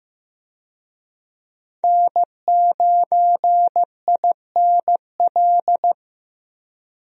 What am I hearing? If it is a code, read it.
Morse code, heard as a keyed tone, about 15 wpm, 700 Hz. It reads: N9INL
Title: USA Call Signs - 5 Characters - Repeated 3x - 15wpm